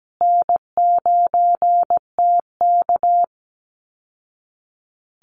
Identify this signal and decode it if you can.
Morse code, heard as a keyed tone, about 17 wpm, 700 Hz. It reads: N9TK
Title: Easy Call Signs Repeated 3x - 17wpm